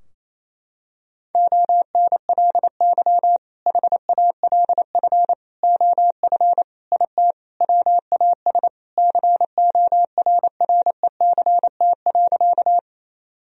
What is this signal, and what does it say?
Morse code, heard as a keyed tone, about 28 wpm, 700 Hz. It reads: ONLY HALF OF IT WAS CORRECT.
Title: Sentences from Top 1000 Words - Review Part 2 - 28wpm